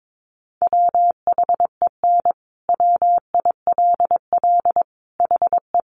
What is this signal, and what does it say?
Morse code, heard as a keyed tone, about 22 wpm, 700 Hz. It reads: WHEN WILL HE
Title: Sentences from Top 200 Words 22wpm